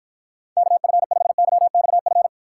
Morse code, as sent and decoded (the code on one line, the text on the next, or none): -.- ..-. ..... -.-- -..- ..-.
KF5YXF